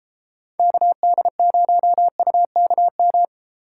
Morse code, heard as a keyed tone, about 32 wpm, 700 Hz. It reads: KD0UKM